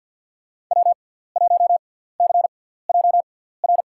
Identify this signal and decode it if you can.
Morse code, heard as a keyed tone, about 48 wpm, spaced as for 17 wpm, 700 Hz. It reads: W1CJR
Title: USA Call Signs - 6x Character Spacing - 50wpm